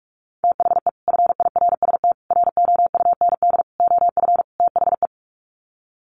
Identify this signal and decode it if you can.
Morse code, heard as a keyed tone, about 44 wpm, 700 Hz. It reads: THE FIRST ROUND OF THE
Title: Sets of 5 Words Encore 45wpm